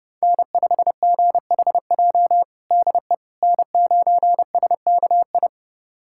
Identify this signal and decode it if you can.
Morse code, heard as a keyed tone, about 30 wpm, 700 Hz. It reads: N5GHJ DE N9SKI
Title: USA Callsign DE Callsign - 5 Characters - 30wpm